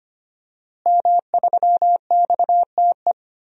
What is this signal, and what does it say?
Morse code, heard as a keyed tone, about 25 wpm, 700 Hz. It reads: M3XTE
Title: DXCC-100 Call Signs - 5 Characters - Repeated 3x - 25wpm